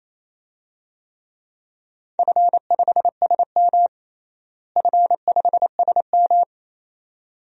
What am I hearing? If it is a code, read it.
Morse code, heard as a keyed tone, about 27 wpm, 700 Hz. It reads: F5SM F5SM
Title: DXCC-100 Call Signs - 4 Characters - Repeated 2x - 28wpm